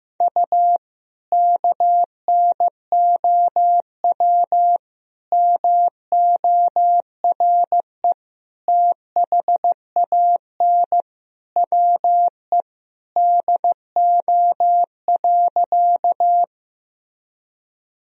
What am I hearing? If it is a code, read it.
Morse code, heard as a keyed tone, about 16 wpm, 700 Hz. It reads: U KNOW MORE THAN WE DO.
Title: Sentences from Top 100 Words and Abbreviations 15wpm